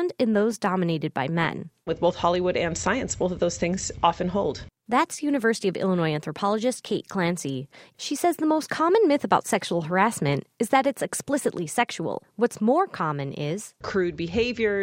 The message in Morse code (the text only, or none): none